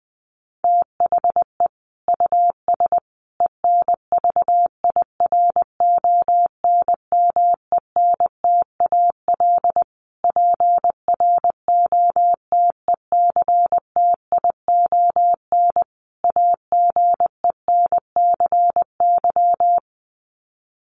Morse code, heard as a keyed tone, about 20 wpm, 700 Hz. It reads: THE US ENVIRONMENTAL PROTECTION AGENCY
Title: Sets of 5 Words Encore2 20wpm